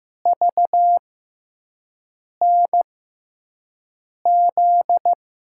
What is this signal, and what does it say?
Morse code, heard as a keyed tone, about 15 wpm, spaced as for 5 wpm, 700 Hz. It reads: VNZ
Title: USA Call Signs - 6x Character Spacing - 15wpm